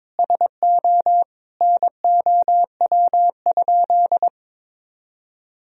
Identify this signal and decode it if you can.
Morse code, heard as a keyed tone, about 23 wpm, 700 Hz. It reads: SO NOW?
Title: Sentences from Top 200 Words and Abbreviations 22wpm